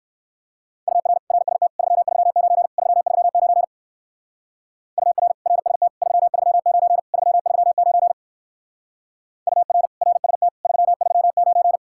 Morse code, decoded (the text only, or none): UR RST 339 339 UR RST 339 339 UR RST 339